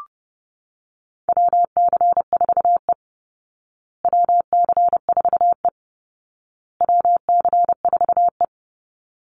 Morse code, heard as a keyed tone, about 30 wpm, 700 Hz. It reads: WC4E WC4E WC4E
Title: Call Signs Repeated 2x - 30wpm